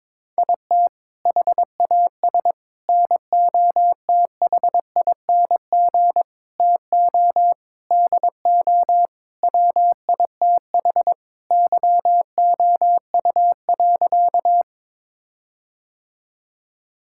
Morse code, heard as a keyed tone, about 22 wpm, 700 Hz. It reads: IT HAS NOTHING TO DO WITH YOU.
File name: practice / 22wpm / Sentences with 4-Word Phrases - 22wpm